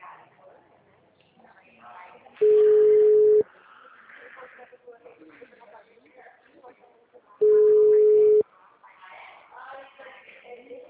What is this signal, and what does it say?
Ringback tone (Europe, 425 Hz), with speech.